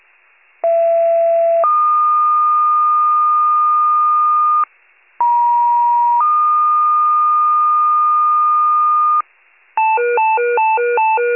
Two-tone page, heard as a one-second tone, then a three-second tone, about 670 then 1220 Hz, about 950 then 1220 Hz. It is heard from 0.6 to 9.2 s.
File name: pager / 83 RESCUE OFC / wcalert